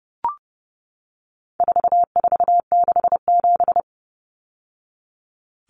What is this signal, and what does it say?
Morse code, heard as a keyed tone, about 30 wpm, 700 Hz. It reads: 4467
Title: Numbers - 4-Digit 30wpm